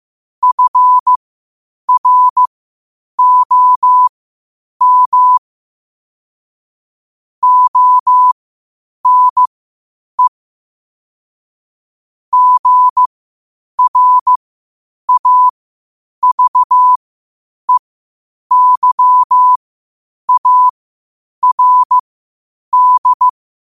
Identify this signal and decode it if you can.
Morse code, heard as a keyed tone, about 15 wpm, spaced as for 8 wpm, 1000 Hz. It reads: FROM ONE GRAVEYARD
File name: qotdfiles / 8wpm